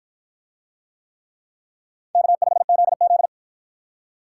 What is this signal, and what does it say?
Morse code, heard as a keyed tone, about 53 wpm, 700 Hz. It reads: K5ZZ